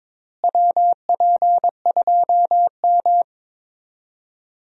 Morse code, heard as a keyed tone, about 22 wpm, 700 Hz. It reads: WP2M
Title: USA Call Signs - 4 Characters - Repeated 3x - 22wpm